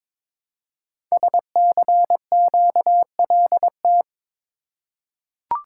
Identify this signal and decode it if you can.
Morse code, heard as a keyed tone, about 22 wpm, 700 Hz. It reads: SCQLT
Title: Code Groups - 5 Letters - 22wpm